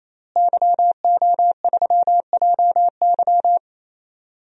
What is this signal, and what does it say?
Morse code, heard as a keyed tone, about 28 wpm, 700 Hz. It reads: YO3JY
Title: DXCC-100 Call Signs - 5 Characters - Repeated 3x - 28wpm